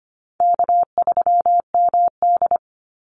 Morse code, heard as a keyed tone, about 25 wpm, 700 Hz. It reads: K3MD
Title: Easy Call Signs Repeated 3x - 25wpm